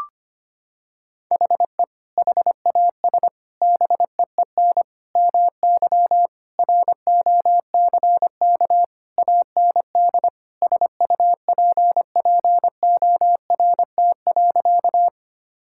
Morse code, heard as a keyed tone, about 25 wpm, 700 Hz. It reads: HE HAS BEEN MY ROCK AND SUPPORT.
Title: Sentences from Top 1000 Words - Review Part 5 - 25wpm